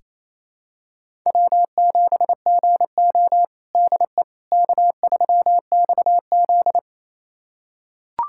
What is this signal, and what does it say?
Morse code, heard as a keyed tone, about 28 wpm, 700 Hz. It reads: W7GO DE K3XZ
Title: USA Callsign DE Callsign - 4 Characters - 28wpm